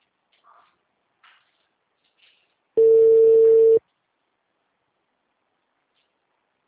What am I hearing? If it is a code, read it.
Ringback tone (China, 450 Hz), with speech.